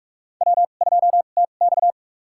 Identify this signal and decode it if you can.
Morse code, heard as a keyed tone, about 44 wpm, spaced as for 32 wpm, 700 Hz. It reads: W2TX
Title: USA Call Signs - 4 Characters - 2x Character Spacing - 45wpm